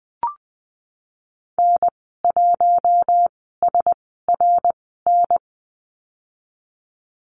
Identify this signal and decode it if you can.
Morse code, heard as a keyed tone, about 20 wpm, spaced as for 14 wpm, 700 Hz. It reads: N1SRN
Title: 062 Lesson 11g - Letter h - Call Signs 20wpm